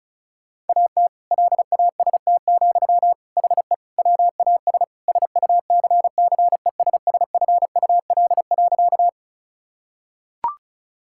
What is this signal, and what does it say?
Morse code, heard as a keyed tone, about 34 wpm, 700 Hz. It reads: AT LAST, HE WAS SUCCESSFUL.
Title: Sentences from Top 1000 Words - Review Part 6 - 35wpm